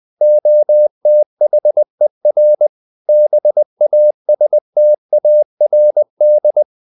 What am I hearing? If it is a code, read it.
Morse code, heard as a keyed tone, about 20 wpm, 600 Hz. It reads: OTHER BASTARD
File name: CW / Practice / EnglishQuotations0320